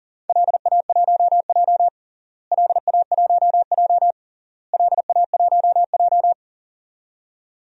Morse code, heard as a keyed tone, about 40 wpm, 700 Hz. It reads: LA1J LA1J LA1J
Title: DXCC-100 Call Signs - 4 Characters - Repeated 3x - 40wpm